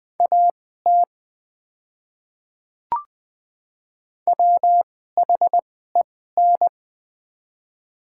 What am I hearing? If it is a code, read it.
Morse code, heard as a keyed tone, about 20 wpm, spaced as for 14 wpm, 690 Hz. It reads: AT WHEN
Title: 2x Character Spacing - Top 100 Words - 20wpm